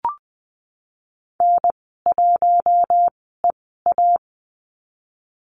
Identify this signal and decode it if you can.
Morse code, heard as a keyed tone, about 19 wpm, spaced as for 14 wpm, 700 Hz. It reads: N1EA
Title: 069 Lesson 12g - Letter d - Call Signs 20wpm